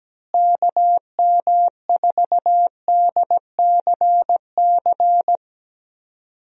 Morse code, heard as a keyed tone, about 17 wpm, 700 Hz. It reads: KM4DCC
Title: USA Call Signs - 6 Characters - Repeated 3x - 17wpm